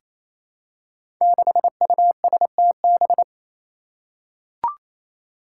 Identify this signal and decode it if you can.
Morse code, heard as a keyed tone, about 28 wpm, 700 Hz. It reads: 6USTB
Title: Five Challenging Letter-Number Combinations 28wpm